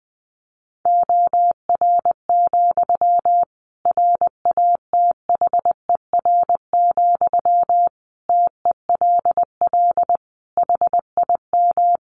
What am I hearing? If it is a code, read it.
Morse code, heard as a keyed tone, about 19 wpm, 700 Hz. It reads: OR, RATHER, TELL HIM
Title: Sentences from Top 700 Words 20wpm